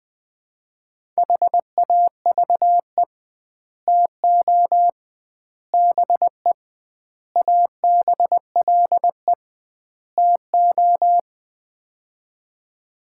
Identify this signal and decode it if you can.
Morse code, heard as a keyed tone, about 20 wpm, 700 Hz. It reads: HAVE TO BE ABLE TO
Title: 2x Word-Spacing - Sets of 5 Words - 20wpm